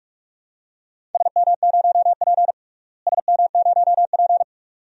Morse code, heard as a keyed tone, about 41 wpm, 700 Hz. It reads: SM0P SM0P
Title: DXCC-100 Call Signs - 4 Characters - Repeated 2x - 45wpm